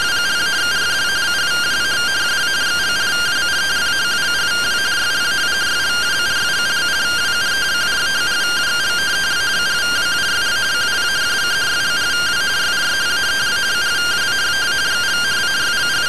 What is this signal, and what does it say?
Two-tone teleprinter signal, about 170 Hz shift.
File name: findings / CT SAGITAL 27.gen-18 rtty rx